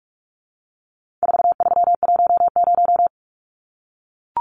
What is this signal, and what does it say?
Morse code, heard as a keyed tone, about 44 wpm, 700 Hz. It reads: 4310
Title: Numbers - 4-Digit 45wpm